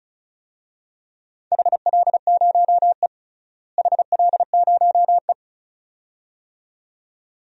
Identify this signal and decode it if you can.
Morse code, heard as a keyed tone, about 34 wpm, 700 Hz. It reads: HL0E HL0E
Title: DXCC-100 Call Signs - 4 Characters - Repeated 2x - 35wpm